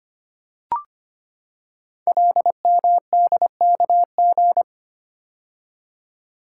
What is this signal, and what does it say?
Morse code, heard as a keyed tone, about 25 wpm, 700 Hz. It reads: LMDKG
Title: Code Groups - 5 Letters - 25wpm